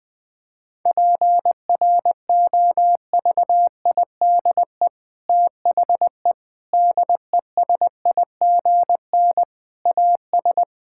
Morse code, heard as a keyed tone, about 20 wpm, 700 Hz. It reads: PROVIDE THE DESIGN AS
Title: Sentences from Top 1000 Words - Review Part 5 - 20wpm